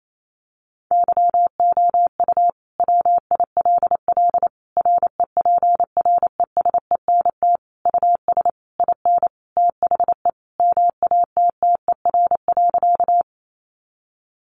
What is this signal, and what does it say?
Morse code, heard as a keyed tone, about 28 wpm, 700 Hz. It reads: YOU WILL REPRESENT US IN THE MATTER.
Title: Sentences from Top 1200 Words 28wpm